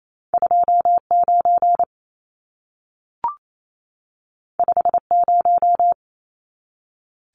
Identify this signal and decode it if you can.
Morse code, heard as a keyed tone, about 28 wpm, 700 Hz. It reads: 29 50
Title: Numbers - 2-Digit 28wpm